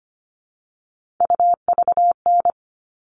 Morse code, heard as a keyed tone, about 25 wpm, 700 Hz. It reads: UVN
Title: All Cut Numbers - 3-Digit 25wpm